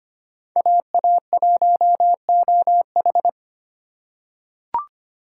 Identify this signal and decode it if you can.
Morse code, heard as a keyed tone, about 25 wpm, 700 Hz. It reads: AA1OH
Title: USA Call Signs - 5 Characters - 3x Character Spacing - 25wpm